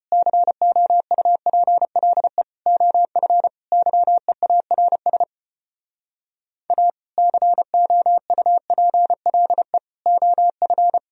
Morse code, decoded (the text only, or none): COUPLE OF YEARS A COUPLE OF